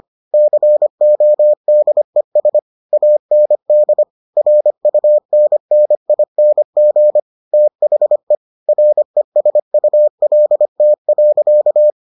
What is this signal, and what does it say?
Morse code, heard as a keyed tone, about 25 wpm, 600 Hz. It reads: CODES AND RUNNING THE RESULT.